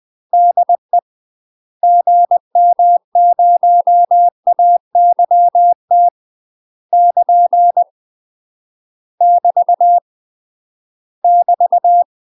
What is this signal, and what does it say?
Morse code, heard as a keyed tone, about 20 wpm, 700 Hz. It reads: DE GM0AYT <KN> <BT> <BT>
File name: UK MASTER QSOs TRAINING/20WPM